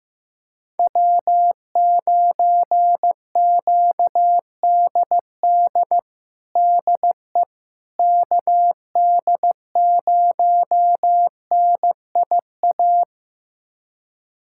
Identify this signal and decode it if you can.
Morse code, heard as a keyed tone, about 16 wpm, 700 Hz. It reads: W9QDD DE KD0NIA
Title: USA Callsign DE Callsign 15wpm